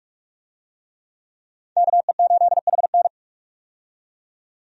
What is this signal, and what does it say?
Morse code, heard as a keyed tone, about 46 wpm, 700 Hz. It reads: KE8HN
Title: USA Call Signs - 5 Characters - 45wpm